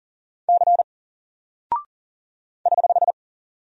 Morse code, heard as a keyed tone, about 40 wpm, 700 Hz. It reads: C <HH>
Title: Single Character (Letter:Number:Punctuation:Prosign) 40wpm